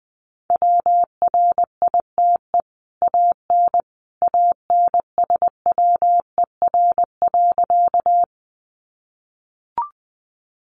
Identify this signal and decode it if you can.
Morse code, heard as a keyed tone, about 19 wpm, 700 Hz. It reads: WRITE AN ANSWER.
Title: Sentences from Top 300 Words 20wpm